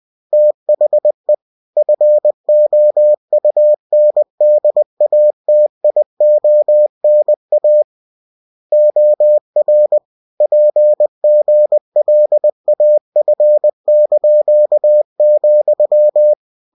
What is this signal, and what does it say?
Morse code, heard as a keyed tone, about 20 wpm, 600 Hz. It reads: THE FOUNDATIONA OR PGLAF),
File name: cw-books / winnie-the-pooh / winnie-20wpm-page-0114